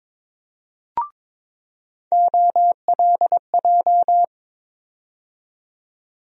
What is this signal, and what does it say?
Morse code, heard as a keyed tone, about 22 wpm, 700 Hz. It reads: OLJ
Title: USA Call Sign Postfixes - 3 Letters 22wpm